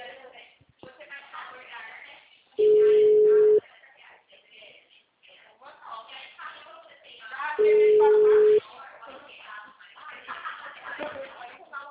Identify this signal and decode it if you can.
Ringback tone (Europe, 425 Hz), with speech.
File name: out- -2012- - - .199